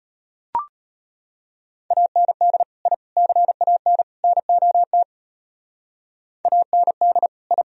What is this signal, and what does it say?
Morse code, heard as a keyed tone, about 31 wpm, 700 Hz. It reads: AND I CAN NOT AND I
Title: Speed-Racing - Sets of 4 Words 25wpm